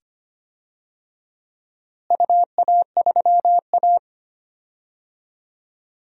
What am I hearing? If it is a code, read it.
Morse code, heard as a keyed tone, about 25 wpm, 700 Hz. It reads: UA3A
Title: DXCC-100 Call Signs - 4 Characters - Repeated 2x - 25wpm